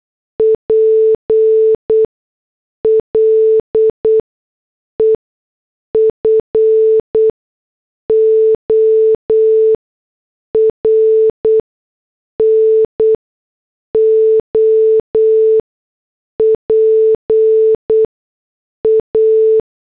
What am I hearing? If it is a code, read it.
Morse code, heard as a keyed tone, about 8 wpm, spaced as for 6 wpm, 440 Hz. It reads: PLE FOR NO PA